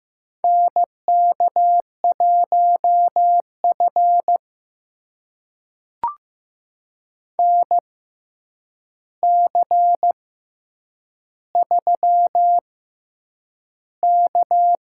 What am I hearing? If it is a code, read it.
Morse code, heard as a keyed tone, about 15 wpm, spaced as for 8 wpm, 700 Hz. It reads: NK1F NC3K
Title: USA Call Signs - 4 Characters - 6x Character Spacing - 15wpm